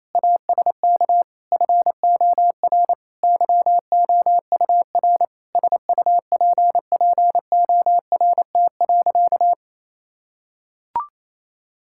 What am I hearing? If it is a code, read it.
Morse code, heard as a keyed tone, about 28 wpm, 700 Hz. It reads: ASK FOR YOUR SUPPORT.
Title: Sentences from Top 1000 Words - Review Part 5 - 28wpm